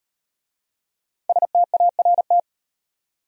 Morse code, heard as a keyed tone, about 38 wpm, 700 Hz. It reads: START